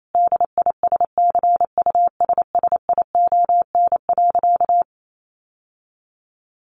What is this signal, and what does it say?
Morse code, heard as a keyed tone, about 28 wpm, 700 Hz. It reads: DISCUSSION.
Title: Sentences from Top 1500 Words 28wpm